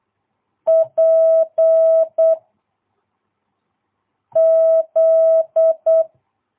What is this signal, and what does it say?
Morse code, heard as a keyed tone, about 8 wpm, 660 Hz. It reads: P Z
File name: jh8cbh / 2022cw